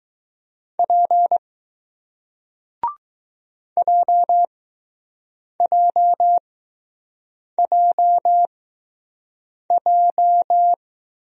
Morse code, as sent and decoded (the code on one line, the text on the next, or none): .--. / .--- .--- .--- .---
P JJJJ